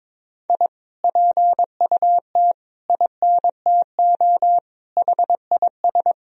Morse code, heard as a keyed tone, about 21 wpm, 700 Hz. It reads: I PUT INTO HIS
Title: Sentences from Top 1000 Words - Review Part 2 - 22wpm